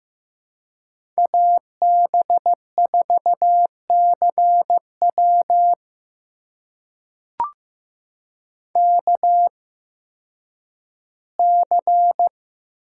Morse code, heard as a keyed tone, about 15 wpm, spaced as for 10 wpm, 700 Hz. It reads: AB4CW KC
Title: USA Call Signs - 5 Characters - 8x Character Spacing - 15wpm